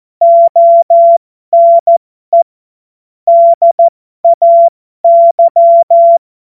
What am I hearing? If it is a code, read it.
Morse code, heard as a keyed tone, about 14 wpm, spaced as for 12 wpm, 680 Hz. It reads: ONE DAY
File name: Morsekurs / Witze